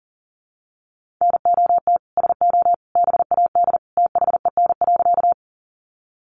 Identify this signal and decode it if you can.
Morse code, heard as a keyed tone, about 40 wpm, 700 Hz. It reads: NOT SO BAD THEN.
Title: Sentences from Top 700 Words 40wpm